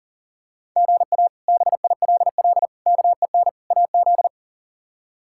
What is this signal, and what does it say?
Morse code, heard as a keyed tone, about 40 wpm, 690 Hz. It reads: GA BILL KEN AZ
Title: K1USN SST QSO Practice - 40wpm